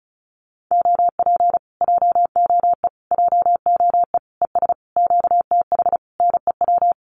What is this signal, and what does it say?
Morse code, heard as a keyed tone, about 36 wpm, 700 Hz. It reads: OP JOE JOE ES QTH NEW
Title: Common QSO Phrases Encore 35wpm